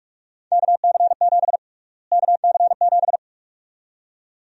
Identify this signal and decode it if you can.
Morse code, heard as a keyed tone, about 46 wpm, 700 Hz. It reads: KC7 KC7
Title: USA Call Sign Prefixes - Repeated 2x - 45wpm